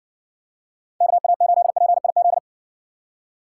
Morse code, heard as a keyed tone, about 60 wpm, 700 Hz. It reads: KA8PIZ